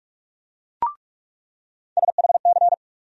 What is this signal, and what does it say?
Morse code, heard as a keyed tone, about 46 wpm, 700 Hz. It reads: SHC
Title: USA Call Sign Postfixes - 3 Letters 45wpm